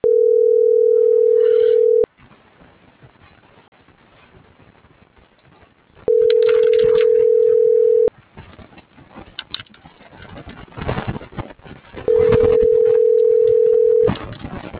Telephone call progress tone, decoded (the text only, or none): ringback tone